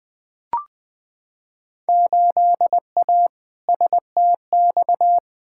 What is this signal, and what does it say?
Morse code, heard as a keyed tone, about 20 wpm, 700 Hz. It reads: 8A STX
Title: Field Day Exchanges Encore - 20wpm